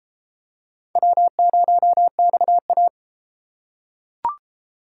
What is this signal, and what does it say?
Morse code, heard as a keyed tone, about 32 wpm, 700 Hz. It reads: W0XA